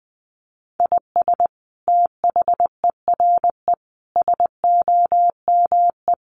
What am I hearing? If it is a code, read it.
Morse code, heard as a keyed tone, about 20 wpm, 700 Hz. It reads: IS THERE SOME
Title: Sentences from Top 200 Words 20wpm